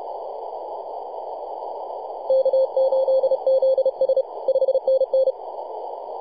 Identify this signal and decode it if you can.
Morse code, heard as a keyed tone, about 31 wpm, 530 Hz, with radio noise. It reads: K8ZS 5NN
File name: posts / 2015-oct-03-2319z-tx3x-21-023mhz-cw